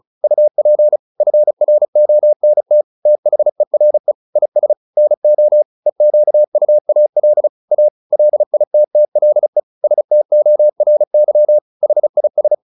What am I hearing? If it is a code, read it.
Morse code, heard as a keyed tone, about 34 wpm, 600 Hz. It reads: UP FRONT THERE IS NO EQUAL A LITTLE STORY HIS